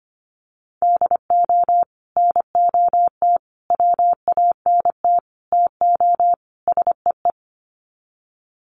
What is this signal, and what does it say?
Morse code, heard as a keyed tone, about 25 wpm, 700 Hz. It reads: DO NOT WANT TO SEE
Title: Sets of 5 Words 25wpm